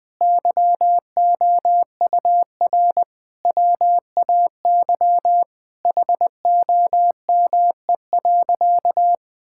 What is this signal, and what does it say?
Morse code, heard as a keyed tone, about 21 wpm, 700 Hz. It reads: YOUR WAY HOME.
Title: Sentences from Top 1600 Words 20wpm